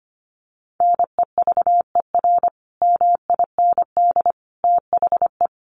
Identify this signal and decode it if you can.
Morse code, heard as a keyed tone, about 25 wpm, 700 Hz. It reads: NEVER MIND THE
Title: Sentences from Top 1400 Words 25wpm